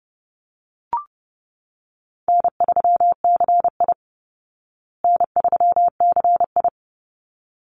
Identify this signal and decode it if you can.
Morse code, heard as a keyed tone, about 30 wpm, 700 Hz. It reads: N3CI N3CI
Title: Easy Call Signs Repeated 2x - 30wpm